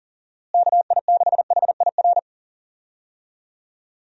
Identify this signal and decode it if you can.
Morse code, heard as a keyed tone, about 37 wpm, 700 Hz. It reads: KI6HIR
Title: USA Call Signs - 6 Characters - Repeated 3x - 40wpm